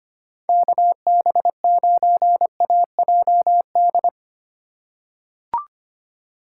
Morse code, heard as a keyed tone, about 25 wpm, 700 Hz. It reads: KB9AJD
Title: USA Call Signs - 4x Character Spacing - 25wpm